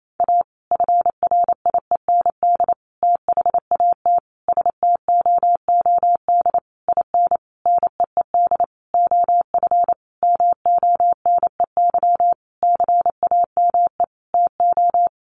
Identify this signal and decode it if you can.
Morse code, heard as a keyed tone, about 28 wpm, 700 Hz. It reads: A FRIEND THAT STOOD IN NEED OF MONEY CAME TO
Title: Sentences from Top 1500 Words 28wpm